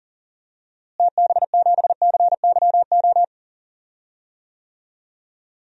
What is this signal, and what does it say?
Morse code, heard as a keyed tone, about 37 wpm, 700 Hz. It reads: TB7CYO
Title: DXCC-100 Call Signs - 6 Characters - Repeated 3x - 40wpm